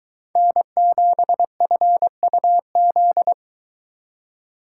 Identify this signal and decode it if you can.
Morse code, heard as a keyed tone, about 23 wpm, 700 Hz. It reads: N7FUZ